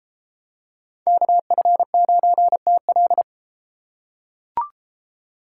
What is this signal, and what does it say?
Morse code, heard as a keyed tone, about 32 wpm, 700 Hz. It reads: KF9TL